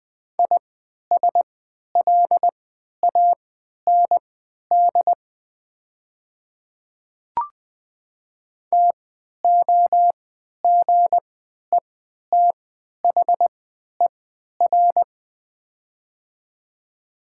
Morse code, heard as a keyed tone, about 20 wpm, spaced as for 11 wpm, 700 Hz. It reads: ISLAND TOGETHER
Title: 3x Character Spacing - Top 500 Words - 20wpm